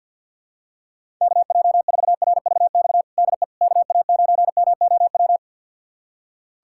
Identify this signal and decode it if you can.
Morse code, heard as a keyed tone, about 51 wpm, 700 Hz. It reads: KJ4RVX DE KA9MOW
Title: USA Callsign DE Callsign - 6 Characters - 50wpm